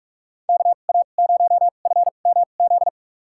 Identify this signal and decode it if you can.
Morse code, heard as a keyed tone, about 45 wpm, spaced as for 32 wpm, 700 Hz. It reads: KA0FMZ
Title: USA Call Signs - 2x Character Spacing - 45wpm